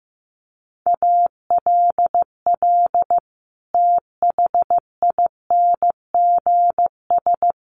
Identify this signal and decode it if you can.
Morse code, heard as a keyed tone, about 15 wpm, 700 Hz. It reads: ALL THINGS
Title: Sets of 5 Words Encore 15wpm